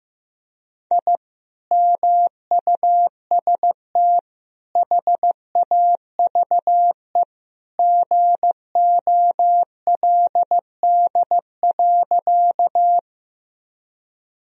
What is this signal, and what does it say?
Morse code, heard as a keyed tone, about 15 wpm, 700 Hz. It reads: I MUST HAVE GOLD.
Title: Sentences from Top 1000 Words - Review Part 3 - 15wpm